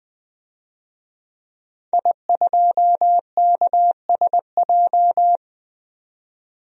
Morse code, heard as a keyed tone, about 20 wpm, 700 Hz. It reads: I2KSJ
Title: DXCC-100 Call Signs - 5 Characters - 20wpm